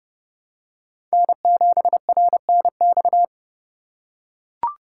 Morse code, heard as a keyed tone, about 31 wpm, 700 Hz. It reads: N7RNX
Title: USA Call Signs - 3x Character Spacing - 30wpm